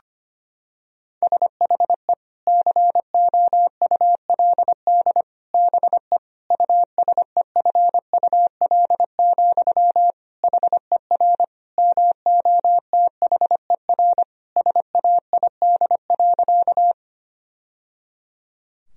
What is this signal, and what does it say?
Morse code, heard as a keyed tone, about 25 wpm, 700 Hz. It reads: SHE COULD BE USEFUL, HER MOTHER SAID.
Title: Sentences from Top 1000 Words - Review Part 4 - 25wpm